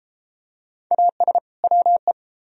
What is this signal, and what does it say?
Morse code, heard as a keyed tone, about 30 wpm, 700 Hz. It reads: AS WE